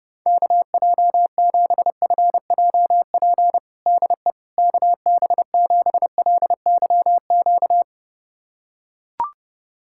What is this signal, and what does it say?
Morse code, heard as a keyed tone, about 30 wpm, 700 Hz. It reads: KJ7FJP DE KB7LYQ
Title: USA Callsign DE Callsign - 6 Characters - 30wpm